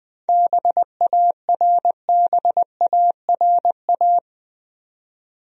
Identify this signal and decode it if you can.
Morse code, heard as a keyed tone, about 20 wpm, 700 Hz. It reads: BARBARA
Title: USA Common Names - 20wpm